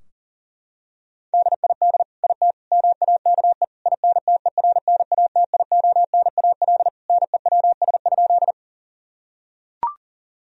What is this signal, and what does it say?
Morse code, heard as a keyed tone, about 40 wpm, 700 Hz. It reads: DID IT MAKE INTERNATIONAL NEWS?
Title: Sentences from Top 1000 Words - Review Part 4 - 40wpm